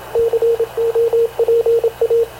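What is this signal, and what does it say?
Morse code, heard as a keyed tone, about 26 wpm, 460 Hz, with radio noise. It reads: COPA
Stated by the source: Havana Radio CLA press (PX) broadcast, recorded by Radio Officer Paul Dumesnil aboard mv CNAV QUEST/CZDO.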